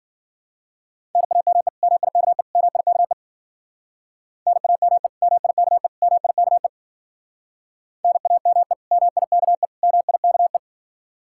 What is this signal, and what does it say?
Morse code, heard as a keyed tone, about 50 wpm, 700 Hz. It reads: NAME MIKE MIKE NAME MIKE MIKE NAME MIKE MIKE
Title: Speed-Racing Common QSO Phrases - 40wpm